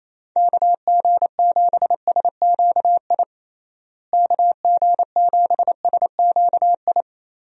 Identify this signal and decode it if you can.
Morse code, heard as a keyed tone, about 27 wpm, 700 Hz. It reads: KG7SQI KG7SQI
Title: USA Call Signs - 6 Characters - Repeated 3x - 28wpm